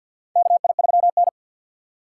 Morse code, heard as a keyed tone, about 51 wpm, 700 Hz. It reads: KI3N
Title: USA Call Signs - 4x Character Spacing - 50wpm